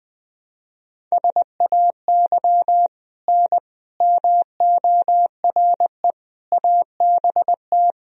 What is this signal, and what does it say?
Morse code, heard as a keyed tone, about 20 wpm, 700 Hz. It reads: SAY N MORE ABT
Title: Sentences from Top 200 Words and Abbreviations 20wpm